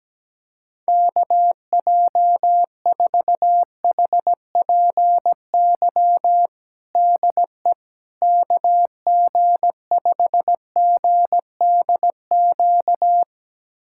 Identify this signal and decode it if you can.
Morse code, heard as a keyed tone, about 17 wpm, 700 Hz. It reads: KJ4HPY DE KG5GDQ
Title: USA Callsign DE Callsign 17wpm